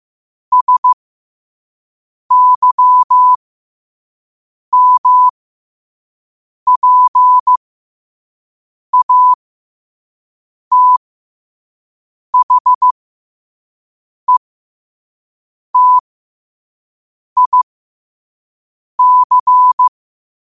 Morse code, heard as a keyed tone, about 15 wpm, spaced as for 5 wpm, 1000 Hz. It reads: SYMPATHETIC